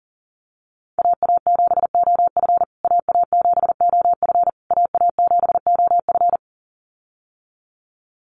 Morse code, heard as a keyed tone, about 40 wpm, 700 Hz. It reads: AA7OF AA7OF AA7OF
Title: USA Call Signs Repeated 3x 40wpm